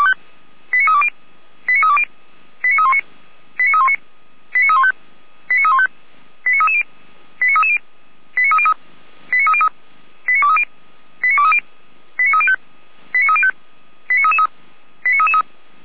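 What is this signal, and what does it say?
Five-tone selective call (ZVEI), heard as quick runs of five tones, tones of about 70 ms. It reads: 89320 89320 89319 89319 89315 89315 89330 89330 89303 89303 89230 89230 89305 89305 89333 89333
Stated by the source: ZVEI five-tone selective call 89110 received by SDR, fire brigade alerting (Feuerwehr Orlamünde).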